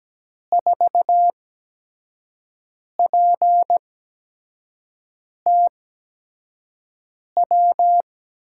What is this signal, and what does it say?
Morse code, heard as a keyed tone, about 17 wpm, 700 Hz. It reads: 4 P T W
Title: USA Call Signs - 6 Characters - 8x Character Spacing - 17wpm